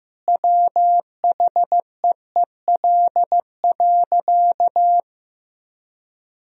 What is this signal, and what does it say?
Morse code, heard as a keyed tone, about 15 wpm, 700 Hz. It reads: WHEEL.
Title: Sentences from Top 1000 Words - Review Part 3 - 15wpm